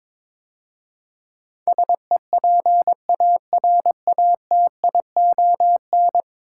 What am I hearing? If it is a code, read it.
Morse code, heard as a keyed tone, about 22 wpm, 700 Hz. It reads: SEPARATION
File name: practice / 22wpm / 10 Letter Words - First 3 Letters Given - 22wpm